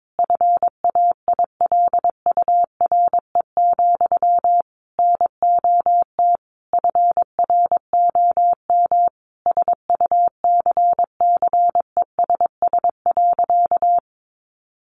Morse code, heard as a keyed tone, about 22 wpm, 700 Hz. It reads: FAILURE, NOT FROM SUCCESS.
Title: Sentences from Top 900 Words 22wpm